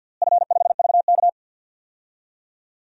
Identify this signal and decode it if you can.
Morse code, heard as a keyed tone, about 53 wpm, 700 Hz. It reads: F5VK
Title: DXCC-100 Call Signs - 4 Characters - Repeated 2x - 50wpm